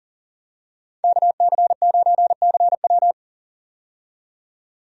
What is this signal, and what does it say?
Morse code, heard as a keyed tone, about 40 wpm, 700 Hz. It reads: KC9CW
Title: USA Call Signs - 5 Characters - 40wpm